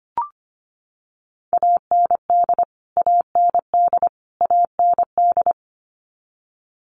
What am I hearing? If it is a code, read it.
Morse code, heard as a keyed tone, about 25 wpm, 700 Hz. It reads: AND AND AND
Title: Top 200 Words Repeated 25wpm